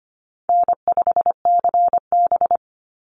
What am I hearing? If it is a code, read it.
Morse code, heard as a keyed tone, about 25 wpm, 700 Hz. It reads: N5CB
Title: USA Call Signs 25wpm